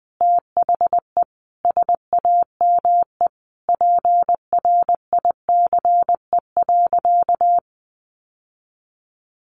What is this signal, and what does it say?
Morse code, heard as a keyed tone, about 20 wpm, 700 Hz. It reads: THE SAME PRICE.
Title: Sentences from Top 600 Words 20wpm